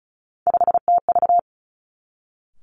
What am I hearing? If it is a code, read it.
Morse code, heard as a keyed tone, about 36 wpm, 700 Hz. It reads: 5TV
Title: Three Challenging Letter-Number Combinations 35wpm